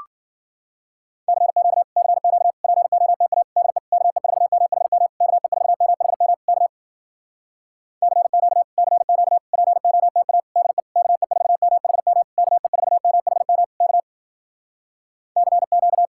Morse code, heard as a keyed tone, about 58 wpm, 700 Hz. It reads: CQ CQ POTA DE KE4MHM KE4MHM K CQ CQ POTA DE KE4MHM KE4MHM K CQ